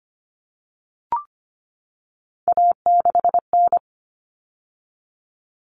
Morse code, heard as a keyed tone, about 25 wpm, 700 Hz. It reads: A6N
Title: All Cut Numbers - 3-Digit 25wpm